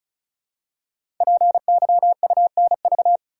Morse code, heard as a keyed tone, about 36 wpm, 700 Hz. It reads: PYUNV
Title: Code Groups - 5 Letters - 35wpm